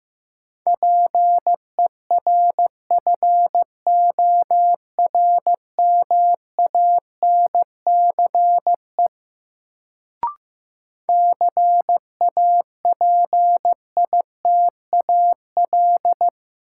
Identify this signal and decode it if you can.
Morse code, heard as a keyed tone, about 15 wpm, 700 Hz. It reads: PERFORMANCE CAPITAL
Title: Sentences from Top 1000 Words - New Word Drill - 15wpm